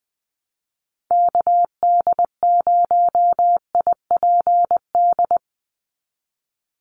Morse code, heard as a keyed tone, about 20 wpm, 700 Hz. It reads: KD0IPD
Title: USA Call Signs 20wpm